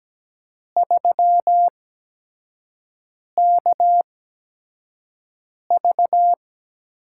Morse code, heard as a keyed tone, about 17 wpm, spaced as for 5 wpm, 700 Hz. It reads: 3KV
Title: USA Call Signs - 4 Characters - 8x Character Spacing - 17wpm